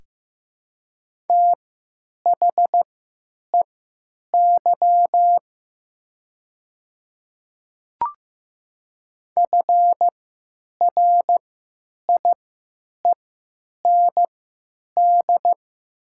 Morse code, heard as a keyed tone, about 15 wpm, spaced as for 9 wpm, 700 Hz. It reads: THEY FRIEND
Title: 3x Character Spacing - Top 300 Words - 15wpm